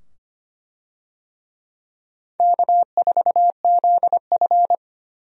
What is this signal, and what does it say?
Morse code, heard as a keyed tone, about 25 wpm, 700 Hz. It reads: K4ZF
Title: USA Call Signs - 4 Characters - 25wpm